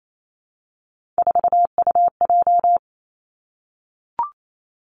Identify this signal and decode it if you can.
Morse code, heard as a keyed tone, about 28 wpm, 700 Hz. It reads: 4UJ
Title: Three Challenging Letter-Number Combinations 28wpm